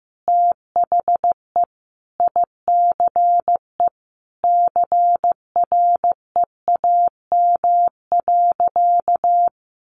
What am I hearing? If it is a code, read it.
Morse code, heard as a keyed tone, about 15 wpm, 700 Hz. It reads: THE ICE CREAM.
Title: Sentences from Top 1500 Words 15wpm